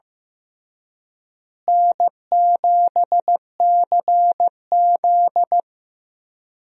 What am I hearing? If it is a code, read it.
Morse code, heard as a keyed tone, about 15 wpm, 700 Hz. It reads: N7CZ
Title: USA Call Signs - 4 Characters - Repeated 3x - 15wpm